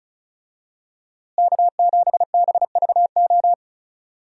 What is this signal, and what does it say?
Morse code, heard as a keyed tone, about 34 wpm, 700 Hz. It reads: K7BVO
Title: USA Call Signs - 5 Characters - Repeated 2x - 35wpm